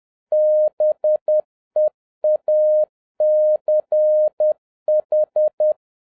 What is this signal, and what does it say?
Morse code, heard as a keyed tone, about 10 wpm, 610 Hz. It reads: BEACH
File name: 2011-07-mp3 / n31 THE 3.11 2 10wpm